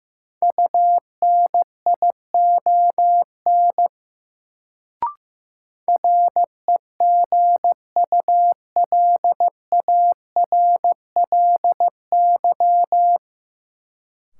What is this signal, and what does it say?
Morse code, heard as a keyed tone, about 16 wpm, 700 Hz. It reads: UNION REGULARLY
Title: Sentences from Top 1400 Words - New Word Drill - 15wpm